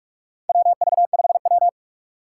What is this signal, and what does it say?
Morse code, heard as a keyed tone, about 43 wpm, 700 Hz. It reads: WV5W
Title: USA Call Signs - 4 Characters - 3x Character Spacing - 45wpm